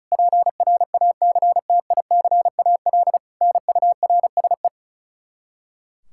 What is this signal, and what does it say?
Morse code, heard as a keyed tone, about 34 wpm, 700 Hz. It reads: PRACTICAL NURSE
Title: Practice QSO - Exchange 3 - More Info - 35wpm